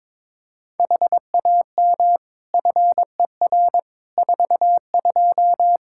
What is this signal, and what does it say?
Morse code, heard as a keyed tone, about 22 wpm, 700 Hz. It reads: HAM FER 42
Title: Practice QSO - Exchange 3 - More Info - 22wpm